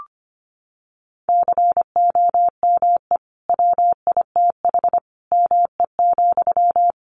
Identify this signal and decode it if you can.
Morse code, heard as a keyed tone, about 25 wpm, 700 Hz. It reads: COME WITH ME,
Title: Sentences from Top 200 Words 25wpm